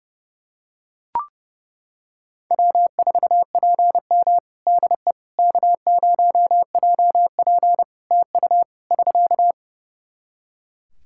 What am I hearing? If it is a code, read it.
Morse code, heard as a keyed tone, about 32 wpm, 700 Hz. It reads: W4PM DE K0JP TU <SK>
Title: Practice QSO - Exchange 4 - Ending - 30wpm